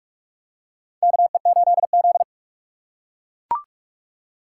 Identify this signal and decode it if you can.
Morse code, heard as a keyed tone, about 44 wpm, 700 Hz. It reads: KE8Z
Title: USA Call Signs - 4 Characters - 3x Character Spacing - 45wpm